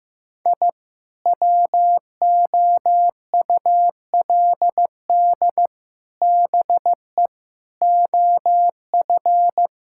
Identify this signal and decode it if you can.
Morse code, heard as a keyed tone, about 15 wpm, 700 Hz. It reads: I WOULD BE OF
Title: Sentences from Top 1000 Words - Review Part 5 - 15wpm